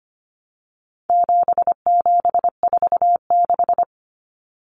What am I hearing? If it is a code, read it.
Morse code, heard as a keyed tone, about 25 wpm, 700 Hz. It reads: 7746
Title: Numbers - 4-Digit 25wpm